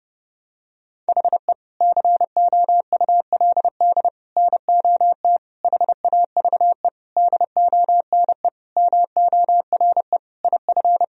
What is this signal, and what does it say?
Morse code, heard as a keyed tone, about 32 wpm, 700 Hz. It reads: HE COULD NOT HAVE DONE MORE IF
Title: Sentences from Top 1000 Words - Review Part 2 - 30wpm